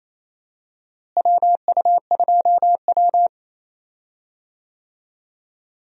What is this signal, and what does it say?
Morse code, heard as a keyed tone, about 28 wpm, 700 Hz. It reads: WU2W
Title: USA Call Signs - 4 Characters - Repeated 3x - 28wpm